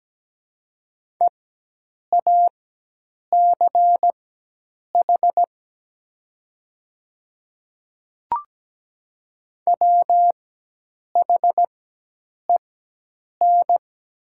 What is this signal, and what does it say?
Morse code, heard as a keyed tone, about 17 wpm, spaced as for 8 wpm, 700 Hz. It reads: EACH WHEN
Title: 4x Character Spacing - Top 100 Words - 17wpm